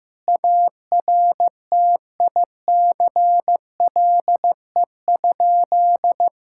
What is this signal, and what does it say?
Morse code, heard as a keyed tone, about 15 wpm, 700 Hz. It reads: ARTICLE?
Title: Sentences from Top 1000 Words - Review Part 5 - 15wpm